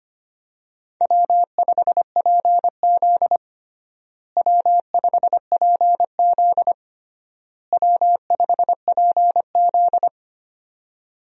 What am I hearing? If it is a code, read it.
Morse code, heard as a keyed tone, about 25 wpm, 700 Hz. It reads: W5PZ W5PZ W5PZ
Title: USA Call Signs - 4 Characters - Repeated 3x - 25wpm